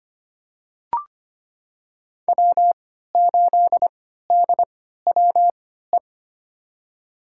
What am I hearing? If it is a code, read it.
Morse code, heard as a keyed tone, about 25 wpm, spaced as for 14 wpm, 700 Hz. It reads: W8DWE
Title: USA Call Signs - 5 Characters - 3x Character Spacing - 25wpm